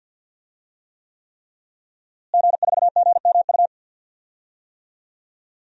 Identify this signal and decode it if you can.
Morse code, heard as a keyed tone, about 48 wpm, 700 Hz. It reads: G4GMU
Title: DXCC-100 Call Signs - 5 Characters - Repeated 3x - 50wpm